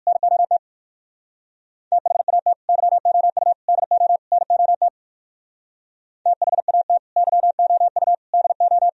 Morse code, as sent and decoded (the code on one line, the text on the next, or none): -. --- - / - .... .- - / -.-- --- ..- / -.. --- / -. --- - / - .... .- - / -.-- --- ..- / -.. ---
NOT THAT YOU DO NOT THAT YOU DO